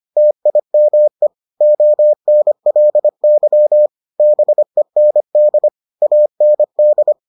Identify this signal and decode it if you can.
Morse code, heard as a keyed tone, about 25 wpm, 600 Hz. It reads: TIME ONLY BEND AND